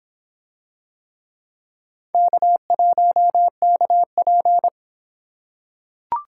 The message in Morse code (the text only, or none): K1KP